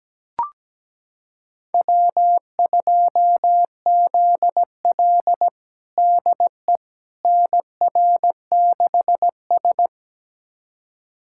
Morse code, heard as a keyed tone, about 17 wpm, 700 Hz. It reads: W2ZL DE NR6S
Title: USA Callsign DE Callsign - 4 Characters - 17wpm